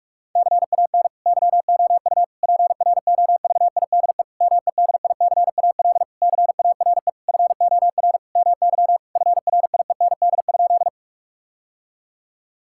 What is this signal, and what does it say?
Morse code, heard as a keyed tone, about 46 wpm, 700 Hz. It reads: CAN YOU PROVIDE MEDICAL CARE FOR MY FRIEND?
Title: Sentences from Top 1000 Words - Review Part 5 - 45wpm